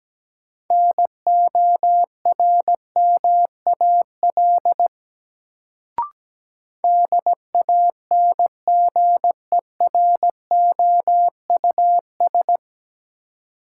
Morse code, heard as a keyed tone, about 17 wpm, 700 Hz. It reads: NORMAL DANGEROUS
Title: Sentences from Top 1200 Words - New Word Drill Alt - 17wpm